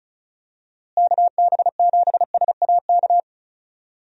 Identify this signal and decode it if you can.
Morse code, heard as a keyed tone, about 36 wpm, 700 Hz. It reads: KB7SAK
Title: USA Call Signs - 6 Characters - Repeated 2x - 35wpm